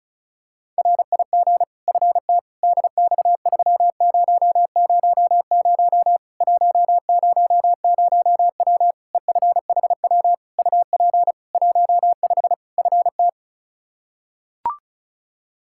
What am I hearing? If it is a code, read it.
Morse code, heard as a keyed tone, about 34 wpm, 700 Hz. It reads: RIG FT DX3000 100W EFHW UP 15 FT
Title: Practice QSO - Exchange 2 - Info - 35wpm